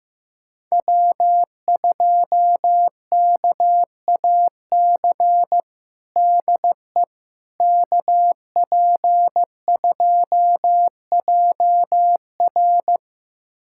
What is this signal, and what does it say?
Morse code, heard as a keyed tone, about 15 wpm, 700 Hz. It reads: W2KAC DE KP2JR
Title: USA Callsign DE Callsign - 5 Characters - 15wpm